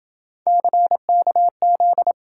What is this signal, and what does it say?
Morse code, heard as a keyed tone, about 26 wpm, 700 Hz. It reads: CKZ